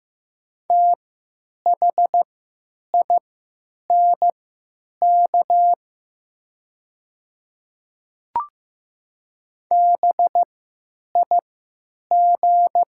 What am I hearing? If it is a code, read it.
Morse code, heard as a keyed tone, about 15 wpm, spaced as for 9 wpm, 700 Hz. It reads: THINK BIG
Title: 3x Character Spacing - Top 300 Words - 15wpm